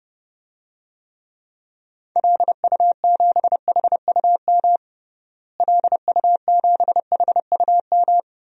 Morse code, heard as a keyed tone, about 30 wpm, 700 Hz. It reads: LU7HUM LU7HUM
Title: DXCC-100 Call Signs - 6 Characters - Repeated 2x - 30wpm